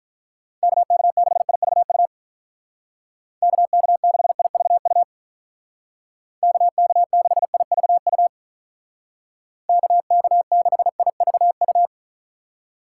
Morse code, heard as a keyed tone, about 45 wpm, 700 Hz. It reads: KK6IVU KK6IVU KK6IVU KK6IVU